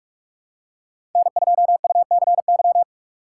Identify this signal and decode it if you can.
Morse code, heard as a keyed tone, about 47 wpm, 700 Hz. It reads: N2UCY
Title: USA Call Signs - 5 Characters - 3x Character Spacing - 45wpm